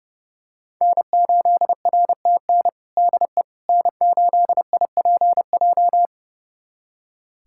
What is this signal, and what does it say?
Morse code, heard as a keyed tone, about 30 wpm, 700 Hz. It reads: N8RTN DE N8IPJ
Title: USA Callsign DE Callsign - 5 Characters - 30wpm